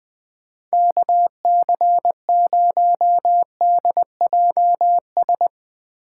Morse code, heard as a keyed tone, about 21 wpm, 700 Hz. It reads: KC0DJS
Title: USA Call Signs - 6 Characters - 4x Character Spacing - 20wpm